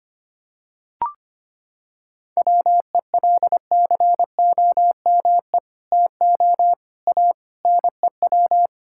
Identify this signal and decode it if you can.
Morse code, heard as a keyed tone, about 25 wpm, 700 Hz. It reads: WELCOME TO A NEW
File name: practice / 25wpm / Sentences with 2-Word Phrases - 25wpm